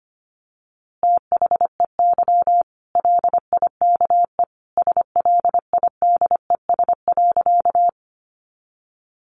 Morse code, heard as a keyed tone, about 25 wpm, 700 Hz. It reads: THEY LIKE SLIDES.
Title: Easy Phrases 25wpm